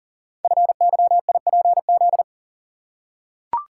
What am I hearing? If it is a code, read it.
Morse code, heard as a keyed tone, about 42 wpm, 700 Hz. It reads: FYIPZ
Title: Code Groups - 5 Letter:Numbers - 40wpm